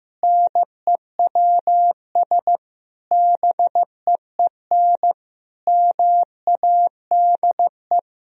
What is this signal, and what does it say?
Morse code, heard as a keyed tone, about 15 wpm, 700 Hz. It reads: NEWS BEEN MADE
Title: Sentences from Top 1000 Words - Review Part 3 - 15wpm